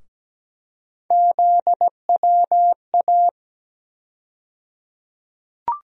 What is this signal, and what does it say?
Morse code, heard as a keyed tone, about 17 wpm, 700 Hz. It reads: ZWA
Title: USA Call Sign Postfixes - 3 Letters 17wpm